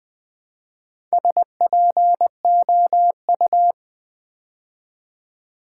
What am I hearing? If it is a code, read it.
Morse code, heard as a keyed tone, about 20 wpm, 700 Hz. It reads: SPOU
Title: Build A Word - 2 Letters At A Time - 6-Letter Words - 20wpm